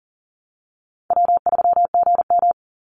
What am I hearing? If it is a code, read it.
Morse code, heard as a keyed tone, about 40 wpm, 700 Hz. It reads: W3GM
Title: USA Call Signs 40wpm